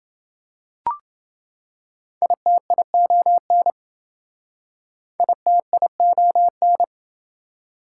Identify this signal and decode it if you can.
Morse code, heard as a keyed tone, about 28 wpm, 700 Hz. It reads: ITION ITION